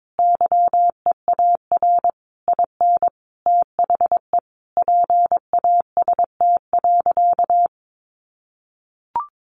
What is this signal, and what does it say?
Morse code, heard as a keyed tone, about 22 wpm, 700 Hz. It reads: YEAR IN THE PAST.
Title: A Sound Of Thunder 22wpm